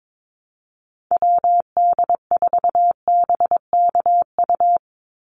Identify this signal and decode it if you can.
Morse code, heard as a keyed tone, about 22 wpm, 700 Hz. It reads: WD4BKU
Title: USA Call Signs 22wpm